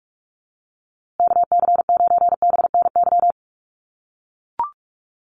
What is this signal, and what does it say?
Morse code, heard as a keyed tone, about 44 wpm, 700 Hz. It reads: KC9BNY
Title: USA Call Signs Repeated 2x 45wpm